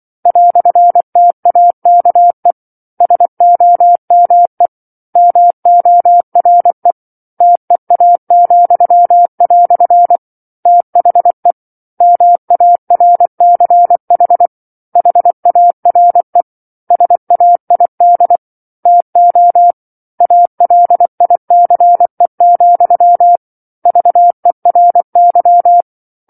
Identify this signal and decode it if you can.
Morse code, heard as a keyed tone, about 24 wpm, 700 Hz. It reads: "TAKE SOME MORE TEA," THE MARCH HARE SAID TO ALICE, VERY
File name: 07-Chapter-07 24 WPM